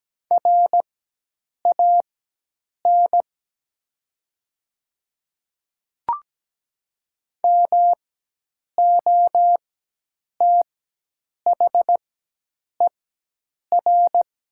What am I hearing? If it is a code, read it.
Morse code, heard as a keyed tone, about 17 wpm, spaced as for 8 wpm, 690 Hz. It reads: RAN MOTHER
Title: 4x Character Spacing - Top 500 Words - 17wpm